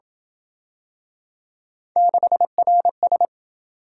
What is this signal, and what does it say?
Morse code, heard as a keyed tone, about 27 wpm, 700 Hz. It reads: 6RS